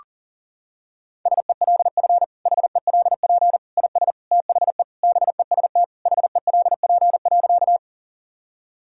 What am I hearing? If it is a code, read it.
Morse code, heard as a keyed tone, about 40 wpm, 700 Hz. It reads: SELF HELP IS THE BEST HELP.
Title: Sentences from Top 1000 Words - Review Part 1 - 40wpm